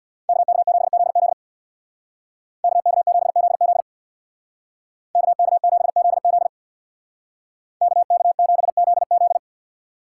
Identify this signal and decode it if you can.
Morse code, heard as a keyed tone, about 64 wpm, 700 Hz. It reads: KK7ZZ KK7ZZ KK7ZZ KK7ZZ